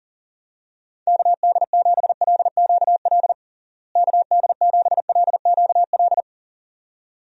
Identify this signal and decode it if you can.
Morse code, heard as a keyed tone, about 40 wpm, 700 Hz. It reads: KD7LQL KD7LQL
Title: USA Call Signs - 6 Characters - Repeated 2x - 40wpm